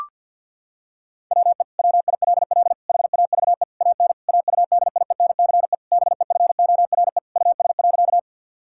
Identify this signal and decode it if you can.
Morse code, heard as a keyed tone, about 53 wpm, 690 Hz. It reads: WE WILL HAVE AN AUDIENCE BEFORE US.
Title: Sentences from Top 1000 Words - Review Part 5 - 50wpm